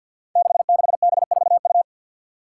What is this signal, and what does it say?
Morse code, heard as a keyed tone, about 46 wpm, 700 Hz. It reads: 66B4U
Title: Five Challenging Letter-Number Combinations 50wpm